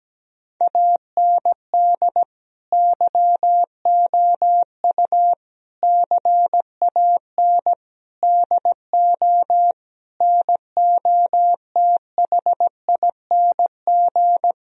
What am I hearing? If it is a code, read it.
Morse code, heard as a keyed tone, about 17 wpm, 700 Hz. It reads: AND YOU CAN DO NOTHING
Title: Sentences from Top 1000 Words - Review Part 1 - 17wpm